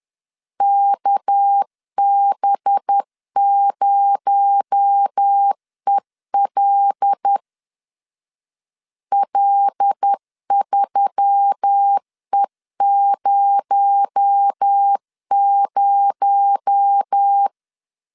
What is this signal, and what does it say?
Morse code, heard as a keyed tone, about 10 wpm, 800 Hz. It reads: KB0EL L3E00
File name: LEZIONI / 6 PKX0 / 6 PKX0 10wpm 2707